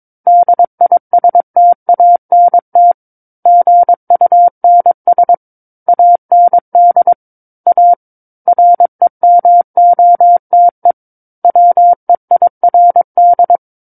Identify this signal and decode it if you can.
Morse code, heard as a keyed tone, about 22 wpm, 700 Hz. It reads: DISTANT GUNS AND A REMOTE WEIRD
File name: Book One / 13-Chapter 13 - 22 WPM